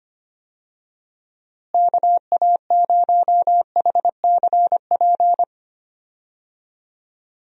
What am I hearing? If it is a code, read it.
Morse code, heard as a keyed tone, about 25 wpm, 700 Hz. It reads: KA0HCP
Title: USA Call Signs - 6 Characters - Repeated 3x - 25wpm